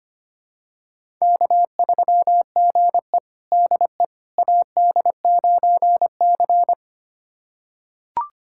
Morse code, heard as a keyed tone, about 25 wpm, 700 Hz. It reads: K3GE DE AD9C
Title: USA Callsign DE Callsign - 4 Characters - 25wpm